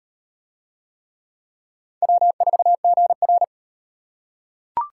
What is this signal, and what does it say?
Morse code, heard as a keyed tone, about 36 wpm, 700 Hz. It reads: W4GR